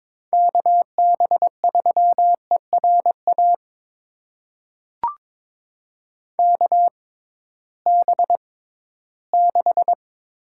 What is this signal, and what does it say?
Morse code, heard as a keyed tone, about 21 wpm, spaced as for 14 wpm, 700 Hz. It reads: KB3ERA KB6
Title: USA Call Signs - 6 Characters - 6x Character Spacing - 22wpm